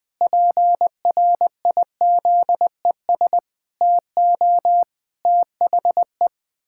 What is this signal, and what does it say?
Morse code, heard as a keyed tone, about 20 wpm, 700 Hz. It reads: PRIZES TO THE
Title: Pangram Challenge v2 - 20wpm